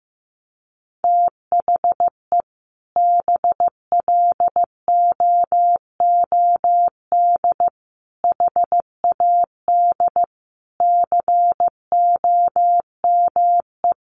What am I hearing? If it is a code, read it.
Morse code, heard as a keyed tone, about 15 wpm, 700 Hz. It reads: THE BLOOD HAD COME
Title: Sentences from Top 1300 Words 15wpm